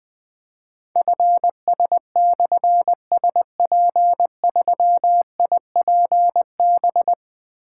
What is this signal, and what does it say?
Morse code, heard as a keyed tone, about 20 wpm, 700 Hz. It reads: FS/SP3IPB
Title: Difficult Call Signs 20wpm